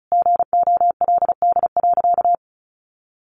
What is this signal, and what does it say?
Morse code, heard as a keyed tone, about 34 wpm, 700 Hz. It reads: GOLD.
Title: Sentences from Top 1500 Words 35wpm